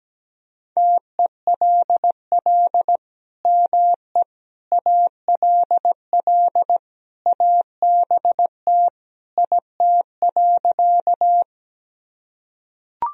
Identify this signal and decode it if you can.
Morse code, heard as a keyed tone, about 17 wpm, 700 Hz. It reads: TELL ME ALL ABT IT.
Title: Sentences from Top 200 Words and Abbreviations 17wpm